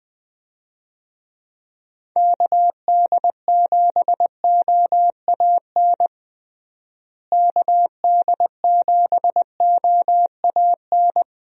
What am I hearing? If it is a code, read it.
Morse code, heard as a keyed tone, about 20 wpm, 700 Hz. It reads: KD7OAN KD7OAN
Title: USA Call Signs - 6 Characters - Repeated 3x - 20wpm